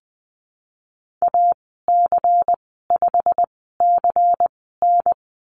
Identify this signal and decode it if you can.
Morse code, heard as a keyed tone, about 20 wpm, spaced as for 14 wpm, 700 Hz. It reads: AC5CN
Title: 154 Lesson 25g - Letter p - Call Signs 20wpm